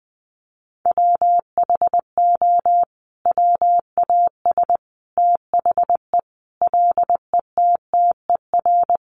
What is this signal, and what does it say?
Morse code, heard as a keyed tone, about 20 wpm, 700 Hz. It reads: WHO WAS THE LETTER
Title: Sentences from Top 300 Words 20wpm